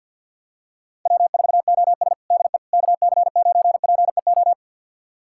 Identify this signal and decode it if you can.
Morse code, heard as a keyed tone, about 46 wpm, 700 Hz. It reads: W4OS DE KC9PEO
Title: USA Callsign DE Callsign 50wpm